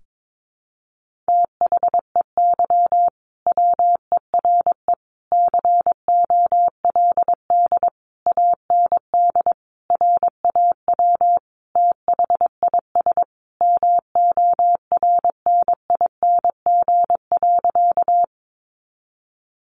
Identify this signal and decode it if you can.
Morse code, heard as a keyed tone, about 22 wpm, 700 Hz. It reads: THEY WERE COLD AND RAW THIS MORNING.
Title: Sentences from Top 1500 Words 22wpm